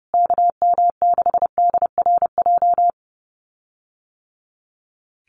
Morse code, heard as a keyed tone, about 30 wpm, 700 Hz. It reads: KM6DRJ
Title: USA Call Signs Repeated 3x 30wpm